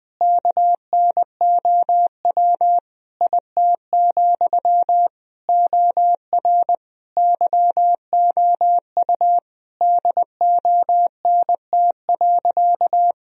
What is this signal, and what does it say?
Morse code, heard as a keyed tone, about 21 wpm, 700 Hz. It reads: KNOW IT, OR YOU DONT.
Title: Sentences from Top 1000 Words - Review Part 4 - 20wpm